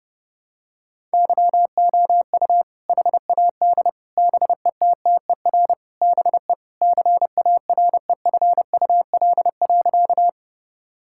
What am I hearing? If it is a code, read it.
Morse code, heard as a keyed tone, about 30 wpm, 700 Hz. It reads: YOU HAD BETTER BE CAREFUL.
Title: Sentences from Top 1600 Words 30wpm